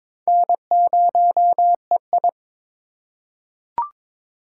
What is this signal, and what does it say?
Morse code, heard as a keyed tone, about 22 wpm, 700 Hz. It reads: N0EI
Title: USA Call Signs - 4 Characters - 4x Character Spacing - 22wpm